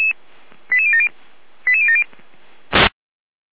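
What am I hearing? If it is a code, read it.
Five-tone selective call (ZVEI), heard as quick runs of five tones, tones of about 70 ms. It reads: 88070 88070
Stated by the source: ZVEI five-tone fire brigade call received on SDR by Feuerwehr Orlamünde.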